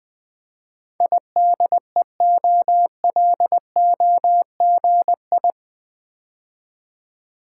Morse code, heard as a keyed tone, about 20 wpm, 700 Hz. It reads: IDEOLOGI
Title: Build A Word - 4 Letters At A Time - 11-Letter Words - 20wpm